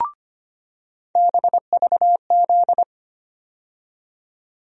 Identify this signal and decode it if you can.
Morse code, heard as a keyed tone, about 25 wpm, 700 Hz. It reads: BVZ
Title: USA Call Sign Postfixes - 3 Letters 25wpm